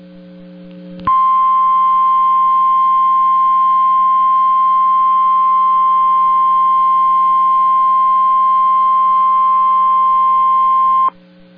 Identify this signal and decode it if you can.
Emergency Alert System (SAME) tones (the 1050 Hz attention tone). It is heard from 1.0 to 11.1 s.